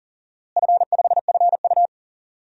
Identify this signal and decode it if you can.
Morse code, heard as a keyed tone, about 40 wpm, 700 Hz. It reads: F5FU
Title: DXCC-100 Call Signs - 4 Characters - Repeated 2x - 40wpm